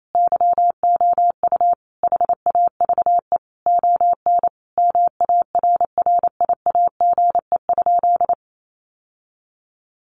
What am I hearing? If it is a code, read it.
Morse code, heard as a keyed tone, about 28 wpm, 700 Hz. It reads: YOU HAVE ON MARRIAGE?
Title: Sentences from Top 1200 Words 28wpm